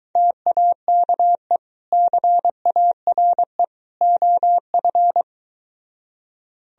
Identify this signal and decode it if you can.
Morse code, heard as a keyed tone, about 24 wpm, 700 Hz. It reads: TAKE CARE OF